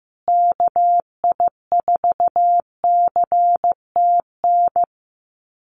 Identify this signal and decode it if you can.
Morse code, heard as a keyed tone, about 15 wpm, 700 Hz. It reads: KI4CTN
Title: USA Call Signs Repeated 2x 15wpm